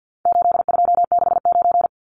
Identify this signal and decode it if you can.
Morse code, heard as a keyed tone, about 50 wpm, 700 Hz. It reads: 8269
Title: Numbers - 4-Digit 50wpm